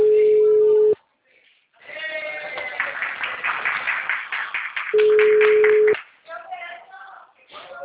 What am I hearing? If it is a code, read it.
Ringback tone (Europe, 425 Hz), with speech.